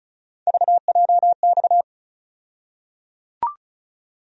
Morse code, heard as a keyed tone, about 36 wpm, 700 Hz. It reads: VJX
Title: USA Call Sign Postfixes - 3 Letters 35wpm